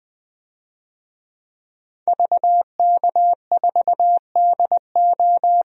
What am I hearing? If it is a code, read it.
Morse code, heard as a keyed tone, about 20 wpm, 700 Hz. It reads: VK4DO
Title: DXCC-100 Call Signs - 5 Characters - 20wpm